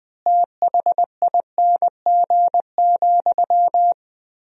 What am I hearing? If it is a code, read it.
Morse code, heard as a keyed tone, about 20 wpm, 700 Hz. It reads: THING,
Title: 3x Word-Spacing - Sentences from Top 200 Words - 20wpm